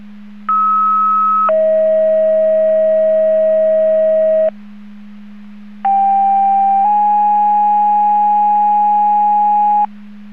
Two-tone page, heard as a one-second tone, then a three-second tone, about 1290 then 640 Hz, about 790 then 830 Hz. It is heard from 0.5 to 9.9 s.